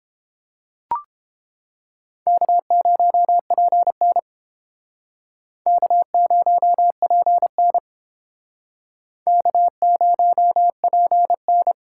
Morse code, heard as a keyed tone, about 30 wpm, 700 Hz. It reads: K0PN K0PN K0PN